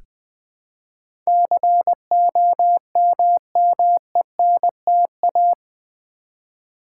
Morse code, heard as a keyed tone, about 20 wpm, 700 Hz. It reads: COMMENTA
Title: Build A Word - 4 Letters At A Time - 11-Letter Words - 20wpm